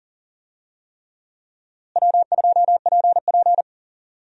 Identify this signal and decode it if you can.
Morse code, heard as a keyed tone, about 40 wpm, 700 Hz. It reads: W2PP
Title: USA Call Signs - 4 Characters - Repeated 3x - 40wpm